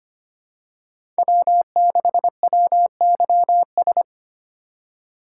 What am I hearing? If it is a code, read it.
Morse code, heard as a keyed tone, about 25 wpm, 700 Hz. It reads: W6WYS
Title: USA Call Signs - 3x Character Spacing - 25wpm